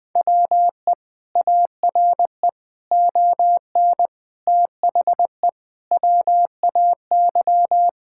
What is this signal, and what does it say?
Morse code, heard as a keyed tone, about 20 wpm, 700 Hz. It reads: WE ARE ON THE WAY
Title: Sentences with 4-Word Phrases - 20wpm